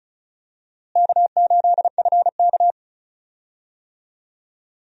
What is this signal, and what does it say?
Morse code, heard as a keyed tone, about 36 wpm, 700 Hz. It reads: K8FK
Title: USA Call Signs - 4 Characters - Repeated 2x - 35wpm